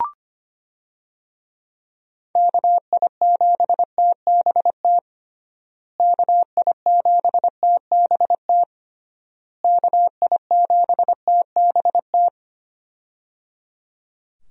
Morse code, heard as a keyed tone, about 25 wpm, 700 Hz. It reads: KI7TBT KI7TBT KI7TBT
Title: USA Call Signs - 6 Characters - Repeated 3x - 25wpm